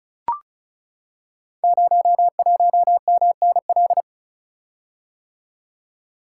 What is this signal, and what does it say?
Morse code, heard as a keyed tone, about 35 wpm, 700 Hz. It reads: 01MNL
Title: Code Groups - 5 Letter:Numbers - 35wpm